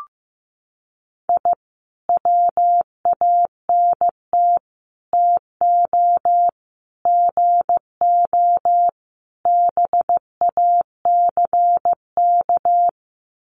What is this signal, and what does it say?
Morse code, heard as a keyed tone, about 15 wpm, 700 Hz. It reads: I WANT TO GO BACK
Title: Sets of 5 Words Encore 15wpm